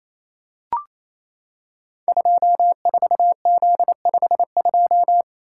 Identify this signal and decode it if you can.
Morse code, heard as a keyed tone, about 29 wpm, 700 Hz. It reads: 24Z52
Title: Code Groups - 5 Letter:Numbers - 28wpm